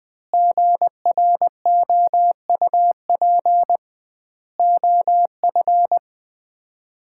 Morse code, heard as a keyed tone, about 20 wpm, 700 Hz. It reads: GROUP OF
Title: 2x Word-Spacing - Sets of 3 Words - 20wpm